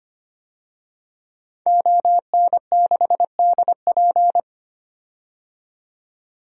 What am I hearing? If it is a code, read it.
Morse code, heard as a keyed tone, about 25 wpm, 700 Hz. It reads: ON6DP
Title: DXCC-100 Call Signs - 5 Characters - 25wpm